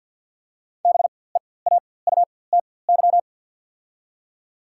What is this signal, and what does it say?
Morse code, heard as a keyed tone, about 53 wpm, spaced as for 24 wpm, 700 Hz. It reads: BEAUTY
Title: 4x Character Spacing - Top 500 Words - 50wpm